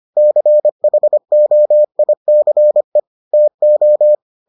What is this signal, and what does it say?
Morse code, heard as a keyed tone, about 25 wpm, 600 Hz. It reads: CHOICE TO